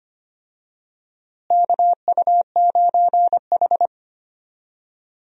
Morse code, heard as a keyed tone, about 25 wpm, 700 Hz. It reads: KU9H
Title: USA Call Signs - 4 Characters - Repeated 2x - 25wpm